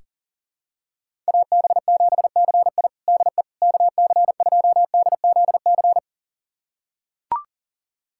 Morse code, heard as a keyed tone, about 37 wpm, 700 Hz. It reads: AB7CI DE KC2DZC
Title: USA Callsign DE Callsign 40wpm